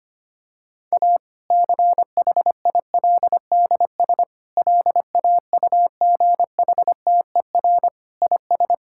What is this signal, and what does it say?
Morse code, heard as a keyed tone, about 25 wpm, 700 Hz. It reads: A CHILDS LAUGHTER IS
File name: practice / 25wpm / Difficult Sentences with Hints - 25wpm